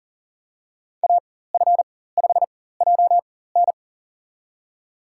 Morse code, heard as a keyed tone, about 40 wpm, spaced as for 19 wpm, 700 Hz. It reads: AF5JN
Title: USA Call Signs - 4x Character Spacing - 40wpm